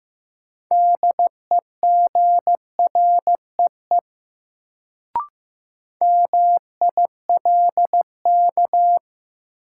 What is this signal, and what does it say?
Morse code, heard as a keyed tone, about 15 wpm, 700 Hz. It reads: DEGREE MILK
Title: Sentences from Top 1500 Words - New Word Drill Alt - 15wpm